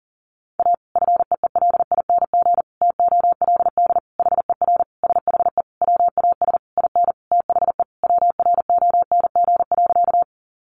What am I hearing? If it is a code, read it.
Morse code, heard as a keyed tone, about 40 wpm, 700 Hz. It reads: A FEELING TOLD HER SHE WAS IN THE WRONG.
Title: Sentences from Top 1200 Words 40wpm